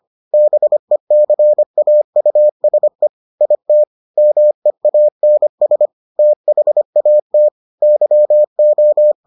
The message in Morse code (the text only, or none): BECAUSE IT MEANS THAT YO